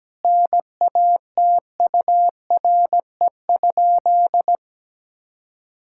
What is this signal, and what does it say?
Morse code, heard as a keyed tone, about 17 wpm, 700 Hz. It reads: NATURE?
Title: Sentences from Top 1000 Words - Review Part 5 - 17wpm